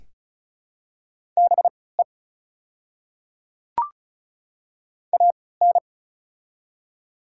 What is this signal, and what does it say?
Morse code, heard as a keyed tone, about 36 wpm, spaced as for 20 wpm, 690 Hz. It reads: BE AN
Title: 3x Character Spacing - Top 100 Words - 35wpm